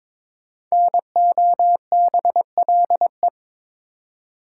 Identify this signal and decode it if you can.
Morse code, heard as a keyed tone, about 22 wpm, 700 Hz. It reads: NOBLE
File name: 5 Letter Words - First 3 Letters Given - 22wpm